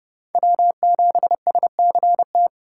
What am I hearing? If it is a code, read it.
Morse code, heard as a keyed tone, about 30 wpm, 700 Hz. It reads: W7SCT
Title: USA Call Signs - 4x Character Spacing - 30wpm